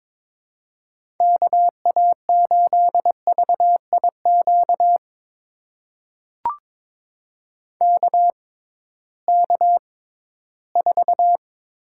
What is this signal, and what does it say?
Morse code, heard as a keyed tone, about 22 wpm, spaced as for 14 wpm, 700 Hz. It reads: KA8VIQ KK4
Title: USA Call Signs - 6 Characters - 6x Character Spacing - 22wpm